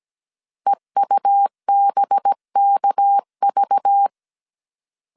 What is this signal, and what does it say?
Morse code, heard as a keyed tone, about 17 wpm, 800 Hz. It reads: EUBKV
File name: 16wpm